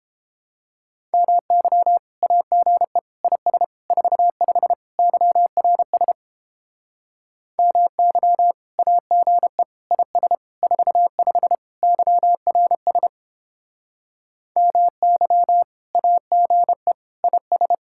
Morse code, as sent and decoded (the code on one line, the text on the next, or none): -- -.-- / .- --. / .. ... / ....- ..... / -.-- .-. ... / -- -.-- / .- --. / .. ... / ....- ..... / -.-- .-. ... / -- -.-- / .- --. / .. ...
MY AGE IS 45 YRS MY AGE IS 45 YRS MY AGE IS